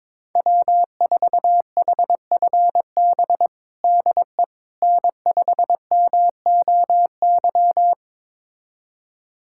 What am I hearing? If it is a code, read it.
Morse code, heard as a keyed tone, about 23 wpm, 700 Hz. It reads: W4HFB DE N5MOY
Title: USA Callsign DE Callsign - 5 Characters - 22wpm